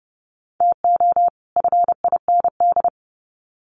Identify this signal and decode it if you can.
Morse code, heard as a keyed tone, about 30 wpm, 700 Hz. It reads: TO FIND
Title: Sets of 2 Words 30wpm